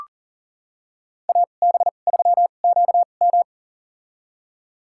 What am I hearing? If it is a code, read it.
Morse code, heard as a keyed tone, about 40 wpm, spaced as for 29 wpm, 700 Hz. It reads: AB3QM
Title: USA Call Signs - 5 Characters - 2x Character Spacing - 40wpm